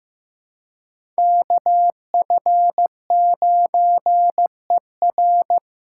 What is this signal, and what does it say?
Morse code, heard as a keyed tone, about 15 wpm, 700 Hz. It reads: KF9ER
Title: USA Callsign DE Callsign 15wpm